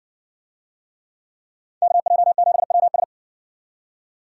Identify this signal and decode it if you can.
Morse code, heard as a keyed tone, about 58 wpm, 700 Hz. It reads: KJ7WS